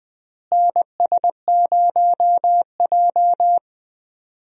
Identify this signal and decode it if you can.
Morse code, heard as a keyed tone, about 21 wpm, 700 Hz. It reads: NS0J
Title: USA Call Signs - 4 Characters - Repeated 2x - 20wpm